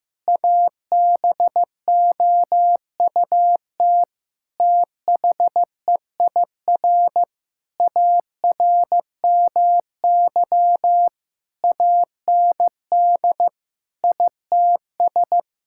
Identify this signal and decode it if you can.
Morse code, heard as a keyed tone, about 15 wpm, 700 Hz. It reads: ABOUT THEIR ARMY AND ITS
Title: Sentences from Top 1000 Words - Review Part 5 - 15wpm